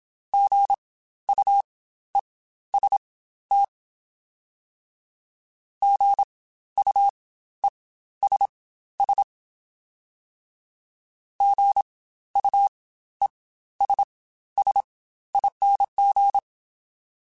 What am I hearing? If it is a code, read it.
Morse code, heard as a keyed tone, about 27 wpm, spaced as for 13 wpm, 800 Hz. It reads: GUEST GUESS GUESSING
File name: Spaced Alphas / Spaced Alphas mp3s / Gs sp @25